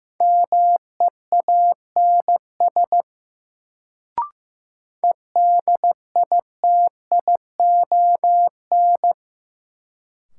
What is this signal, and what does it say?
Morse code, heard as a keyed tone, about 15 wpm, 700 Hz. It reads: MEANS EDITION
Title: Sets of 5 Words - New Word Drill - 15wpm